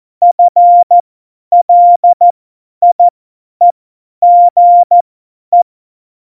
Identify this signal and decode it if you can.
Morse code, heard as a keyed tone, about 14 wpm, spaced as for 10 wpm, 700 Hz. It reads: FLIEGE
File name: Morsekurs / Witze / Witz 09. April 2019, 50 F70